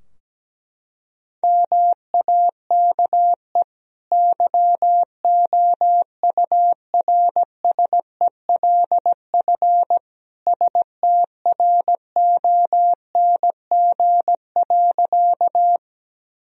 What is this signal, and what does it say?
Morse code, heard as a keyed tone, about 17 wpm, 700 Hz. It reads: MAKE YOURSELF STRONG.
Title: Sentences from Top 1000 Words - Review Part 3 - 17wpm